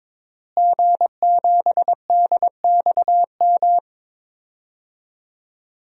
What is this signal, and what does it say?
Morse code, heard as a keyed tone, about 22 wpm, 700 Hz. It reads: G7DXM
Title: DXCC-100 Call Signs - 5 Characters - Repeated 2x - 22wpm